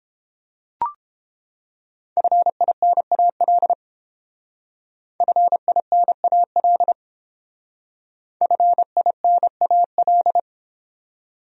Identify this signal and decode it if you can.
Morse code, heard as a keyed tone, about 30 wpm, 700 Hz. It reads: FINAL FINAL FINAL